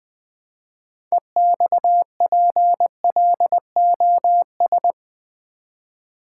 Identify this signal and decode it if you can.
Morse code, heard as a keyed tone, about 21 wpm, 700 Hz. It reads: EXPLOS
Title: Build A Word - 3 Letters At A Time - 9-Letter Words - 20wpm